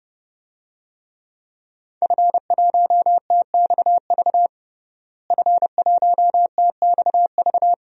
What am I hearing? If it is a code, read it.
Morse code, heard as a keyed tone, about 30 wpm, 700 Hz. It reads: F1TXV F1TXV
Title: DXCC-100 Call Signs - 5 Characters - Repeated 2x - 30wpm